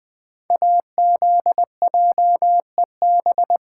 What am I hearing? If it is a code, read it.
Morse code, heard as a keyed tone, about 20 wpm, 700 Hz. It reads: AZJEB
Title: Code Groups - 5 Letter:Numbers - 20wpm